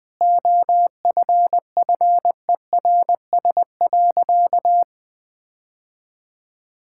Morse code, heard as a keyed tone, about 20 wpm, 700 Hz. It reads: OFFERS.
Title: Sentences with 6-Word Phrases - 20wpm